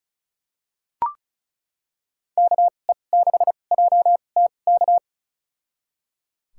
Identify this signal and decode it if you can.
Morse code, heard as a keyed tone, about 36 wpm, spaced as for 25 wpm, 700 Hz. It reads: KE6JTK
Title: USA Call Signs - 6 Characters - 2x Character Spacing - 35wpm